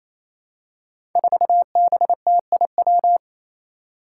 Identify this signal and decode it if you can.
Morse code, heard as a keyed tone, about 27 wpm, 700 Hz. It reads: 4BTIW
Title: Five Challenging Letter-Number Combinations 28wpm